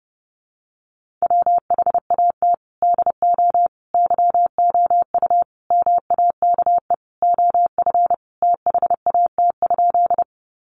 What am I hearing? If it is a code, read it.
Morse code, heard as a keyed tone, about 30 wpm, 700 Hz. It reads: WHAT DO YOU MAKE OF THAT?
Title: Sentences from Top 100 Words 30wpm